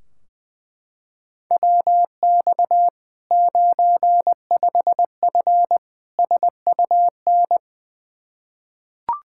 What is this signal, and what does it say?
Morse code, heard as a keyed tone, about 20 wpm, 700 Hz. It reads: WX 95F SUN
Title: Practice QSO - Exchange 2 - Info - 20wpm